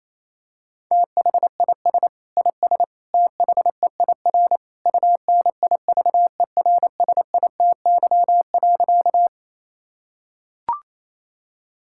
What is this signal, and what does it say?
Morse code, heard as a keyed tone, about 28 wpm, 700 Hz. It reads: THIS IS THEIR UNIVERSITY.
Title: Sentences from Top 1000 Words - Review Part 5 - 28wpm